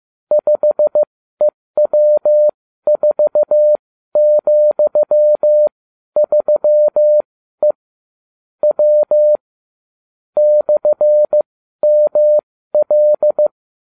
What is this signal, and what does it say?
Morse code, heard as a keyed tone, about 15 wpm, spaced as for 12 wpm, 600 Hz. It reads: 5EW4,3E W /ML